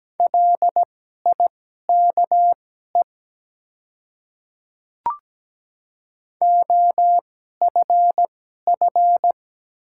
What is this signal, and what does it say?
Morse code, heard as a keyed tone, about 17 wpm, spaced as for 12 wpm, 700 Hz. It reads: LIKE OFF
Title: 2x Character Spacing - Top 300 Words - 17wpm